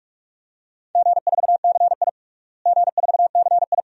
Morse code, heard as a keyed tone, about 44 wpm, 700 Hz. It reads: G4CI G4CI
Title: DXCC-100 Call Signs - 4 Characters - Repeated 2x - 45wpm